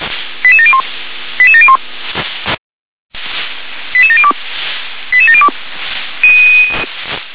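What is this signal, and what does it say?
Five-tone selective call (ZVEI), heard as quick runs of five tones, tones of about 70 ms. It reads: 88701 88701 88702 88702
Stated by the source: ZVEI five-tone selective call received with an SDR.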